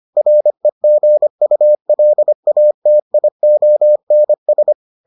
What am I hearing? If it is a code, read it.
Morse code, heard as a keyed tone, about 25 wpm, 600 Hz. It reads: REGULATIONS